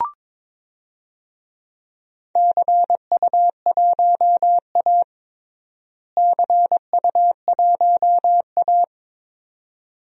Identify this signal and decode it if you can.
Morse code, heard as a keyed tone, about 22 wpm, 700 Hz. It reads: CU1A CU1A
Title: DXCC-100 Call Signs - 4 Characters - Repeated 2x - 22wpm